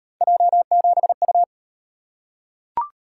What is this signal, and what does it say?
Morse code, heard as a keyed tone, about 38 wpm, 700 Hz. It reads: J7U